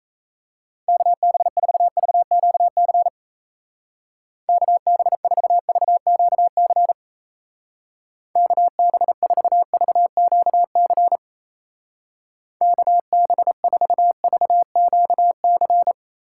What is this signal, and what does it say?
Morse code, heard as a keyed tone, about 33 wpm, 700 Hz. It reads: KB4VQC KB4VQC KB4VQC KB4VQC
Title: Speed Racing - USA 2x3 Call Signs 28wpm